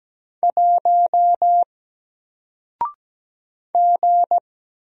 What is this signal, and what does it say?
Morse code, heard as a keyed tone, about 17 wpm, 700 Hz. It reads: 1 G
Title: Single Character (Letter:Number:Punctuation:Prosign) Alt 17wpm